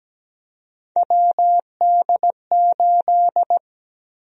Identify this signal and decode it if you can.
Morse code, heard as a keyed tone, about 17 wpm, 700 Hz. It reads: WD8
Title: USA Call Sign Prefixes - Repeated 2x - 17wpm